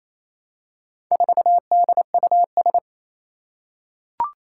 Morse code, heard as a keyed tone, about 27 wpm, 700 Hz. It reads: 4DUS
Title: Four Challenging Letter-Number Combinations 28wpm